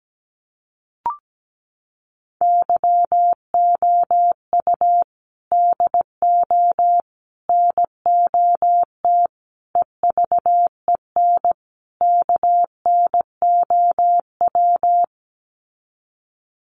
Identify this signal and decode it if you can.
Morse code, heard as a keyed tone, about 17 wpm, 700 Hz. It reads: YOU DO NOT EVEN KNOW
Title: Sets of 5 Words Encore2 17wpm